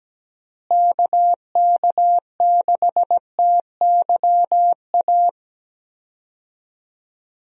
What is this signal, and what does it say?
Morse code, heard as a keyed tone, about 17 wpm, 700 Hz. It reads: KK6TYA
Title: USA Call Signs - 6 Characters - Repeated 2x - 17wpm